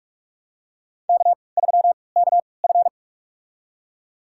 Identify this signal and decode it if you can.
Morse code, heard as a keyed tone, about 46 wpm, spaced as for 25 wpm, 700 Hz. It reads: K3KF
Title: USA Call Signs - 3x Character Spacing - 45wpm